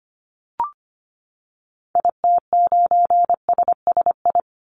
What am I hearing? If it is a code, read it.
Morse code, heard as a keyed tone, about 25 wpm, 700 Hz. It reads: IT9SSI
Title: Call Signs Repeated 3x - 25wpm